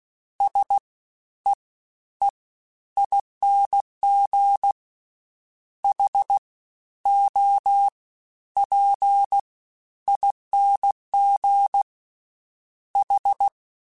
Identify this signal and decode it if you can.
Morse code, heard as a keyed tone, about 16 wpm, 800 Hz. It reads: S E E ING H O P ING H
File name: Misc / Misc-mp3s / Jumps @15